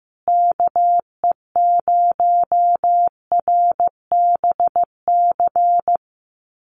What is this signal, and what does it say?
Morse code, heard as a keyed tone, about 15 wpm, 700 Hz. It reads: KE0RBC
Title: USA Call Signs Repeated 2x 15wpm